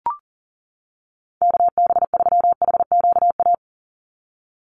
Morse code, heard as a keyed tone, about 40 wpm, 700 Hz. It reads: KB3HQA
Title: USA Call Signs 40wpm